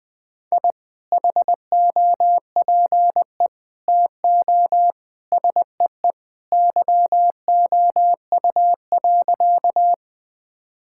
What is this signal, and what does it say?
Morse code, heard as a keyed tone, about 20 wpm, 700 Hz. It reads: I HOPE TO SEE YOU.
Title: Sentences with 2-Word Phrases - 20wpm